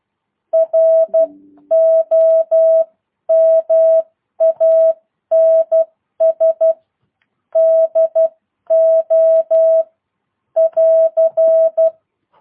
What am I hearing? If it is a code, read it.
Morse code, heard as a keyed tone, about 12 wpm, spaced as for 10 wpm, 650 Hz. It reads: ROMANS DO <AR>